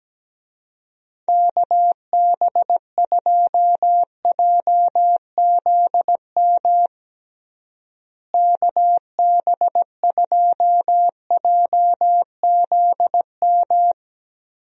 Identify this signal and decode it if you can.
Morse code, heard as a keyed tone, about 17 wpm, 700 Hz. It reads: KB2JZM KB2JZM
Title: USA Call Signs - 6 Characters - Repeated 3x - 17wpm